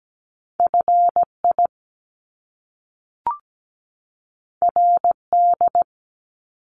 Morse code, heard as a keyed tone, about 17 wpm, 700 Hz. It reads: FI RD
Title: Top Two Letter Combinations in Words 17wpm